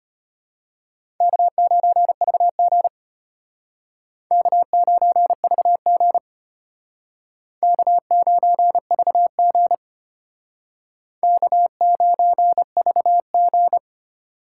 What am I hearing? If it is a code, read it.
Morse code, heard as a keyed tone, about 32 wpm, 700 Hz. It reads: K9VG K9VG K9VG K9VG